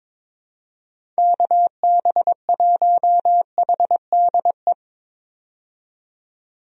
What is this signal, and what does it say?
Morse code, heard as a keyed tone, about 22 wpm, 700 Hz. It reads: KB1HDE
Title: USA Call Signs - 6 Characters - 22wpm